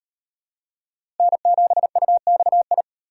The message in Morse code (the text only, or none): N7UXI